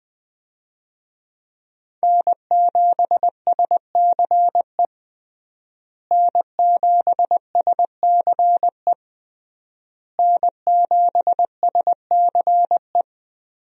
Morse code, heard as a keyed tone, about 20 wpm, 700 Hz. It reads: N7SCE N7SCE N7SCE
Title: USA Call Signs - 5 Characters - Repeated 3x - 20wpm